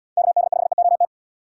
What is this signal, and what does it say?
Morse code, heard as a keyed tone, about 72 wpm, 700 Hz. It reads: KD5EOT